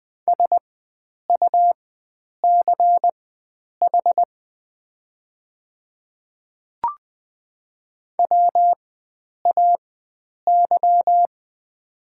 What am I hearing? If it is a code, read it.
Morse code, heard as a keyed tone, about 20 wpm, spaced as for 9 wpm, 700 Hz. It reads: SUCH WAY
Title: 4x Character Spacing - Top 300 Words - 20wpm